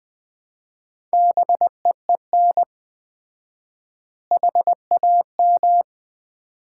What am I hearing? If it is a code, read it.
Morse code, heard as a keyed tone, about 20 wpm, 700 Hz. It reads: BEEN HAM
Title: Common QSO Phrases - 4x Word-Spacing - 20wpm